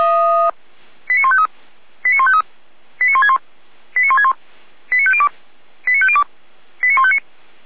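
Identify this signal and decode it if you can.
Five-tone selective call (ZVEI), heard as quick runs of five tones, tones of about 70 ms. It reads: 89153 89153 89161 89161 89502 89502 89159
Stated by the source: ZVEI five-tone call received by SDR, logged by feuerwehr-orlamuende.org (Feuerwehr Orlamünde).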